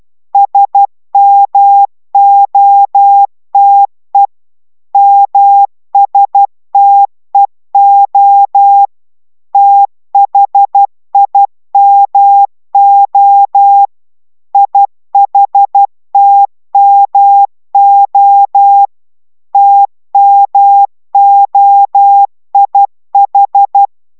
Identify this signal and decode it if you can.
Morse code, heard as a keyed tone, about 12 wpm, 800 Hz. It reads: SMOTE MSTEO THIMO IHTMO TMOIH